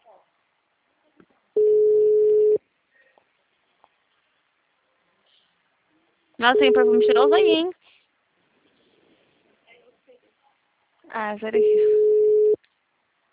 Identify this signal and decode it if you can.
Ringback tone (Europe, 425 Hz), with speech.